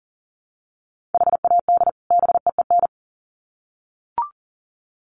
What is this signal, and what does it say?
Morse code, heard as a keyed tone, about 40 wpm, 700 Hz. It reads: HAD BEEN
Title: Sets of 2 Words 40wpm